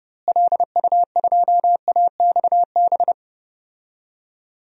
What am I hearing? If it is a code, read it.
Morse code, heard as a keyed tone, about 30 wpm, 700 Hz. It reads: LU2AXB
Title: DXCC-100 Call Signs - 6 Characters - Repeated 3x - 30wpm